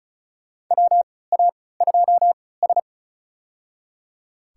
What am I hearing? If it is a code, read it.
Morse code, heard as a keyed tone, about 34 wpm, spaced as for 20 wpm, 700 Hz. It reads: WA2S
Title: USA Call Signs - 4 Characters - 3x Character Spacing - 35wpm